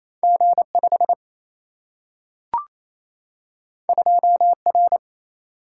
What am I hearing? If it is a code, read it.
Morse code, heard as a keyed tone, about 28 wpm, 700 Hz. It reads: G5 2R
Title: Single-Letter Single-Number 28wpm